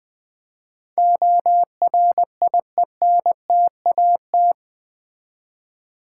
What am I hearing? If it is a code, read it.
Morse code, heard as a keyed tone, about 20 wpm, 700 Hz. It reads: ORIENTAT
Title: Build A Word - 4 Letters At A Time - 11-Letter Words - 20wpm